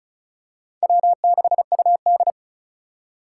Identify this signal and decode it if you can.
Morse code, heard as a keyed tone, about 34 wpm, 700 Hz. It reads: W6UD
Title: USA Call Signs - 4 Characters - 3x Character Spacing - 35wpm